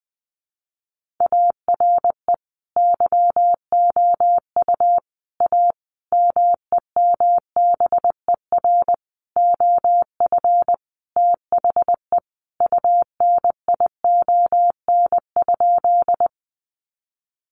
Morse code, heard as a keyed tone, about 20 wpm, 700 Hz. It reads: ARE YOU A MEMBER OF THE UNION?
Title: Sentences from Top 1400 Words 20wpm